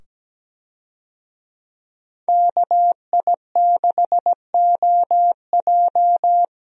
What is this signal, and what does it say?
Morse code, heard as a keyed tone, about 17 wpm, 700 Hz. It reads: KI6OJ
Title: USA Call Signs - 5 Characters - 17wpm